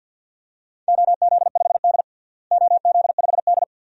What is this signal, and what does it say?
Morse code, heard as a keyed tone, about 51 wpm, 700 Hz. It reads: OZ5D OZ5D
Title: DXCC-100 Call Signs - 4 Characters - Repeated 2x - 50wpm